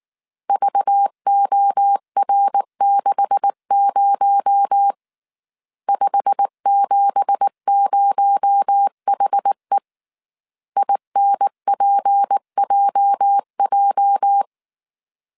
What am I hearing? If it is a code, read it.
Morse code, heard as a keyed tone, about 19 wpm, 800 Hz. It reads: VOR60 570HE INPJJ
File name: CW / DET / 1 9 1500C S A 18wpm 2459